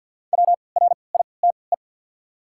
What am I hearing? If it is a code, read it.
Morse code, heard as a keyed tone, about 50 wpm, spaced as for 28 wpm, 700 Hz. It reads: WRITE
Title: 3x Character Spacing - Top 200 Words - 50wpm